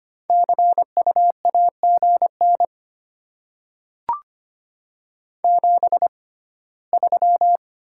Morse code, heard as a keyed tone, about 25 wpm, spaced as for 17 wpm, 700 Hz. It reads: CU AGN 73
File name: Top 20 QSO Elements - 6x Character Spacing - 25wpm